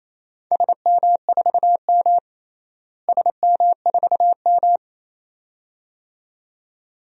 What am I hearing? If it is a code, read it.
Morse code, heard as a keyed tone, about 27 wpm, 700 Hz. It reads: SM4M SM4M
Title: DXCC-100 Call Signs - 4 Characters - Repeated 2x - 28wpm